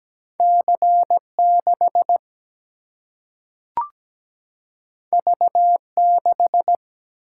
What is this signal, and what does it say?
Morse code, heard as a keyed tone, about 17 wpm, 700 Hz. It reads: C6 V6
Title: Single-Letter Single-Number 17wpm